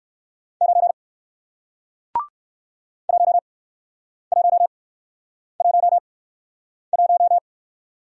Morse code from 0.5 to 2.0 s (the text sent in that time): 9